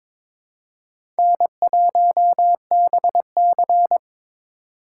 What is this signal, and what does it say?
Morse code, heard as a keyed tone, about 22 wpm, 700 Hz. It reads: N1BC
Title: USA Call Signs - 4 Characters - 4x Character Spacing - 22wpm